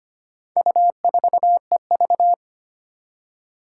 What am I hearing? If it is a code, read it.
Morse code, heard as a keyed tone, about 25 wpm, 700 Hz. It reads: U4EV
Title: Four Challenging Letter-Number Combinations 25wpm